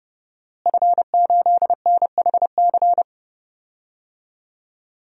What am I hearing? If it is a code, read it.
Morse code, heard as a keyed tone, about 30 wpm, 700 Hz. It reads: F8NHC
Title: DXCC-100 Call Signs - 5 Characters - Repeated 3x - 30wpm